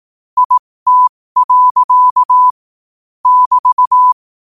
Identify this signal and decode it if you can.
Morse code, heard as a keyed tone, about 18 wpm, spaced as for 15 wpm, 1000 Hz. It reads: IT. <BT>
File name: qotdfiles / 15wpm